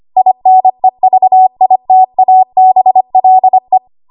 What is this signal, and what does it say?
Morse code, heard as a keyed tone, about 25 wpm, 750 Hz. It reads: INEVITABLE
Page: W1AW Web Code Practice Files